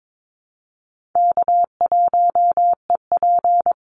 Morse code, heard as a keyed tone, about 22 wpm, 700 Hz. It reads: K1EP
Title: Easy Call Signs Repeated 2x - 22wpm